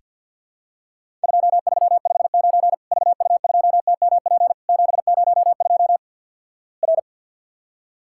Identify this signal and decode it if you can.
Morse code, heard as a keyed tone, about 48 wpm, 700 Hz. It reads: 2359 VU2TMP 702 R
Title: Worked All Europe DX Contest - QTC Exchange with Repeats - 50wpm